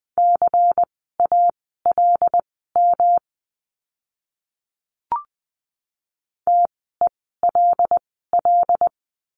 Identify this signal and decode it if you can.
Morse code, heard as a keyed tone, about 20 wpm, spaced as for 14 wpm, 700 Hz. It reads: CALM TELL
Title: 108 Lesson 18f - Letter m - Words 20wpm